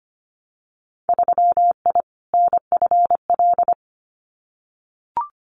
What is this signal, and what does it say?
Morse code, heard as a keyed tone, about 25 wpm, 700 Hz. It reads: 3I NFL
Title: Winter Field Day Exchanges 25wpm